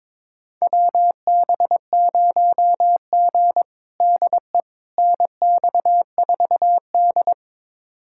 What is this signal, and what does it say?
Morse code, heard as a keyed tone, about 22 wpm, 700 Hz. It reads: WB0G DE NX4D
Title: USA Callsign DE Callsign - 4 Characters - 22wpm